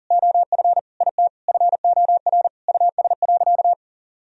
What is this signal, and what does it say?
Morse code, heard as a keyed tone, about 37 wpm, 700 Hz. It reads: OF IT FOR US.
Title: Sentences from Top 1000 Words - Review Part 3 - 40wpm